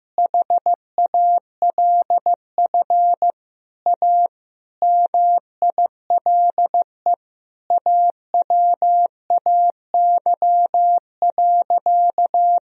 Morse code, heard as a keyed tone, about 15 wpm, 700 Hz. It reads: HALF A MILE AWAY.
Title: Sentences from Top 1000 Words - Review Part 2 - 15wpm